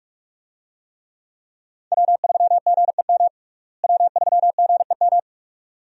Morse code, heard as a keyed tone, about 42 wpm, 700 Hz. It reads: W3GEM W3GEM
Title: USA Call Signs - 5 Characters - Repeated 2x - 45wpm